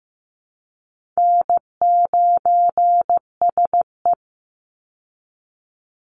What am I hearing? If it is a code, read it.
Morse code, heard as a keyed tone, about 15 wpm, 700 Hz. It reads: N9SE
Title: Easy Call Signs Repeated 2x - 15wpm